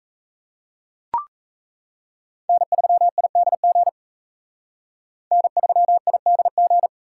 Morse code, heard as a keyed tone, about 35 wpm, 700 Hz. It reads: N3IDG N3IDG
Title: Speed Racing - USA 1x3 Call Signs 28wpm